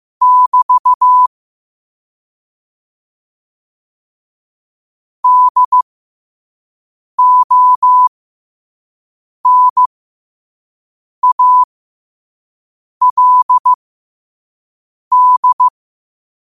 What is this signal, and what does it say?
Morse code, heard as a keyed tone, about 15 wpm, spaced as for 5 wpm, 1000 Hz. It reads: <BT> DONALD
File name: 5wpm